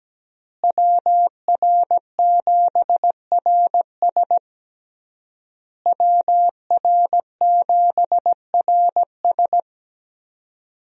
Morse code, heard as a keyed tone, about 17 wpm, 700 Hz. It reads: WR7RS WR7RS
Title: USA Call Signs - 5 Characters - Repeated 3x - 17wpm